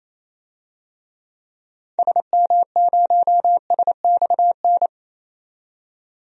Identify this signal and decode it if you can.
Morse code, heard as a keyed tone, about 28 wpm, 700 Hz. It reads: SM0SXN
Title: DXCC-100 Call Signs - 6 Characters - 28wpm